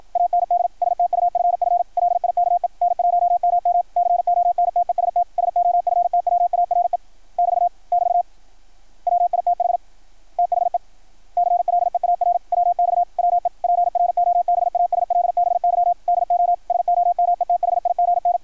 Morse code, heard as a keyed tone, about 53 wpm, 660 Hz. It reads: AND STUFF LIKE N1MM CONTEST SOFTWARE <BT> <BT> WITH THE CLEAR WX WE PROBASLBY DO SOMETHIGN
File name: archived / CRSnet